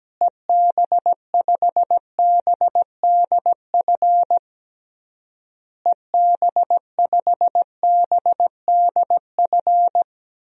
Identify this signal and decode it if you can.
Morse code, heard as a keyed tone, about 17 wpm, 700 Hz. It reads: EB5BDF EB5BDF
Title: DXCC-100 Call Signs - 6 Characters - Repeated 3x - 17wpm